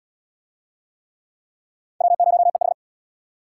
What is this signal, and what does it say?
Morse code, heard as a keyed tone, about 68 wpm, 700 Hz. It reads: W0EH